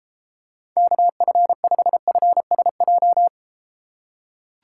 Morse code, heard as a keyed tone, about 33 wpm, 700 Hz. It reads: KF5FSJ